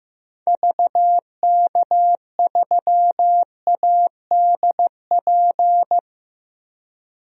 Morse code, heard as a keyed tone, about 15 wpm, 700 Hz. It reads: VK3ADP
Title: DXCC-100 Call Signs - 6 Characters - 15wpm